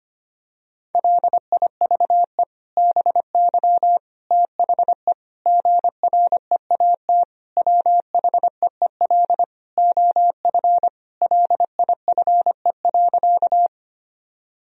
Morse code, heard as a keyed tone, about 25 wpm, 700 Hz. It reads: LIVE BY THE GREAT WHEEL OF LIFE.
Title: Sentences from Top 1000 Words - Review Part 3 - 25wpm